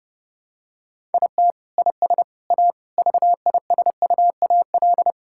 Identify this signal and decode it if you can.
Morse code, heard as a keyed tone, about 30 wpm, 700 Hz. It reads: IT IS A VISUAL
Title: Sentences from Top 1000 Words - Review Part 6 - 30wpm